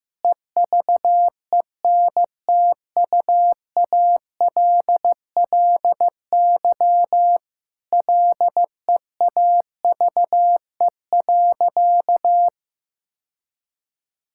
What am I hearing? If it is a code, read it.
Morse code, heard as a keyed tone, about 15 wpm, 700 Hz. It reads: EVENTUALLY LEAVE.
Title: Sentences from Top 1000 Words - Review Part 6 - 15wpm